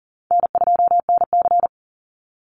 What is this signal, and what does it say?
Morse code, heard as a keyed tone, about 40 wpm, 700 Hz. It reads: N2NC
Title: Easy Call Signs - 40wpm